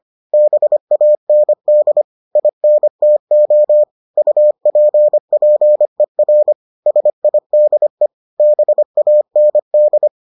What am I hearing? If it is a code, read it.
Morse code, heard as a keyed tone, about 25 wpm, 600 Hz. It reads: BAND INTO UPPER SIDE BAND